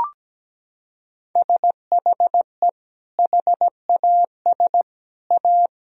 Morse code, heard as a keyed tone, about 17 wpm, 700 Hz. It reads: SHE HAS A
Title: Sentences from Top 1000 Words - Review Part 2 - 17wpm